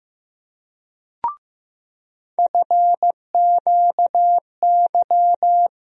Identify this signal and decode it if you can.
Morse code, heard as a keyed tone, about 15 wpm, 700 Hz. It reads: FQY
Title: USA Call Sign Postfixes - 3 Letters 15wpm